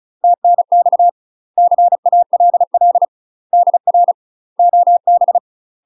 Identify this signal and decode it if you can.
Morse code, heard as a keyed tone, about 36 wpm, 700 Hz. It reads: TNX CALL DR OB
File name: UK MASTER QSOs TRAINING / 35WPM / UK MASTER QSOs 0016